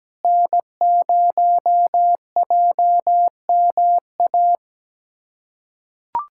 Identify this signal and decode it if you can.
Morse code, heard as a keyed tone, about 17 wpm, 700 Hz. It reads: N0JMA
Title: USA Callsign DE Callsign 17wpm